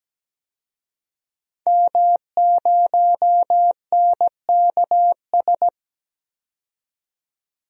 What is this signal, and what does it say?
Morse code, heard as a keyed tone, about 17 wpm, 700 Hz. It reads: M0NKS
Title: DXCC-100 Call Signs - 5 Characters - 17wpm